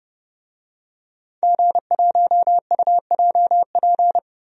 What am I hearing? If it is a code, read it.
Morse code, heard as a keyed tone, about 30 wpm, 700 Hz. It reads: G1UJP
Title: DXCC-100 Call Signs - 5 Characters - Repeated 3x - 30wpm